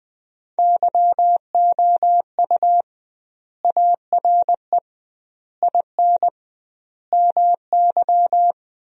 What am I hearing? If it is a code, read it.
Morse code, heard as a keyed tone, about 20 wpm, 700 Hz. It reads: YOU ARE IN MY
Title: 2x Word-Spacing - Sentences from Top 500 Words - 20wpm